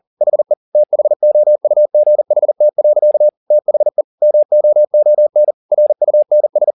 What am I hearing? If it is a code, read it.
Morse code, heard as a keyed tone, about 40 wpm, 600 Hz. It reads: HE THOUGHT. THE MOON RUNS
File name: the old man and the sea 2 40wpm